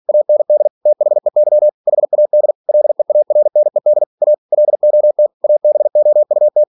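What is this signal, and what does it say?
Morse code, heard as a keyed tone, about 46 wpm, 600 Hz. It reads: AND THEY HAD LEARNED A LOT ABOUT
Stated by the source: Practice text from an English novel sent at 47 WPM.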